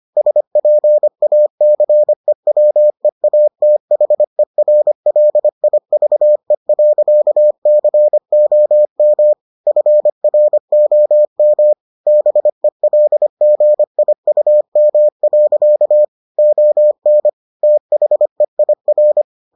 Morse code, heard as a keyed tone, about 25 wpm, 600 Hz. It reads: SPACEWEATHERLIVE.COM FROM BELGIUM. ON THEIR